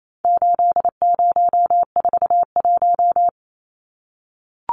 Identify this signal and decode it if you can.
Morse code, heard as a keyed tone, about 28 wpm, 700 Hz. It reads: 8041
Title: Numbers - 4-Digit 28wpm